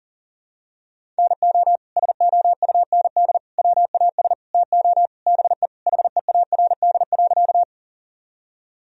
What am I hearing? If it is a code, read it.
Morse code, heard as a keyed tone, about 40 wpm, 700 Hz. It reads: NO SOUND WAS TO BE HEARD.
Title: Sentences from Top 1000 Words - Review Part 1 - 40wpm